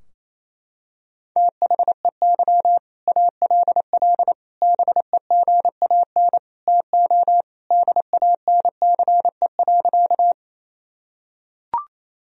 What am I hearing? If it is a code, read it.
Morse code, heard as a keyed tone, about 28 wpm, 700 Hz. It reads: THEY ALL BEGAN TO DANCE.
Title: Sentences from Top 1000 Words - Review Part 4 - 28wpm